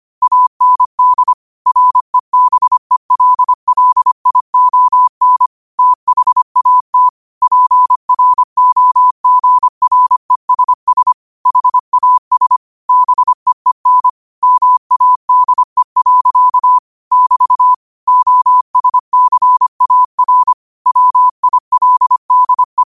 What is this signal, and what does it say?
Morse code, heard as a keyed tone, about 25 wpm, 1000 Hz. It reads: AND REBELLION THAT PROGRESS HAS BEEN MADE. <BT> OSCAR WILDE
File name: qotdfiles / 25wpm